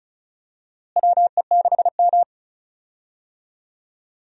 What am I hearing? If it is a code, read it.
Morse code, heard as a keyed tone, about 34 wpm, 700 Hz. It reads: WE6M
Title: USA Call Signs - 4 Characters - Repeated 3x - 35wpm